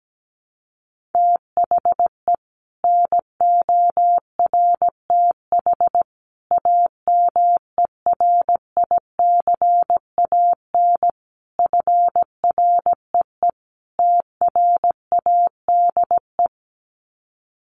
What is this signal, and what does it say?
Morse code, heard as a keyed tone, about 17 wpm, 700 Hz. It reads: THE NORTH AMERICAN FREE TRADE
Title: Sets of 5 Words Encore 17wpm